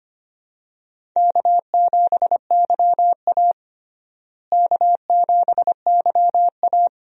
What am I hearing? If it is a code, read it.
Morse code, heard as a keyed tone, about 25 wpm, 700 Hz. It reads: K7YA K7YA
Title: USA Call Signs - 4 Characters - Repeated 2x - 25wpm